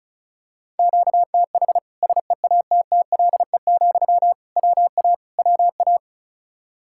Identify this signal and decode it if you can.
Morse code, heard as a keyed tone, about 36 wpm, 700 Hz. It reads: QTH SEATTLE, WA WA
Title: Practice QSO - Exchange 1 - Beginning - 35wpm